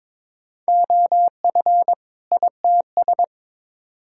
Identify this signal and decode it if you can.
Morse code, heard as a keyed tone, about 22 wpm, 700 Hz. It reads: OF ITS
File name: Build A Phrase - 2-Word Phrases - 22wpm